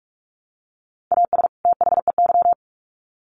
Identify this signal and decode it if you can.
Morse code, heard as a keyed tone, about 44 wpm, 700 Hz. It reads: AS THEY
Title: Sets of 2 Words 45wpm